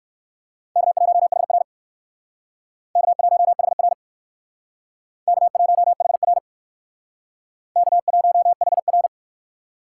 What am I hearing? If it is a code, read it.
Morse code, heard as a keyed tone, about 56 wpm, 700 Hz. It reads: K1HR K1HR K1HR K1HR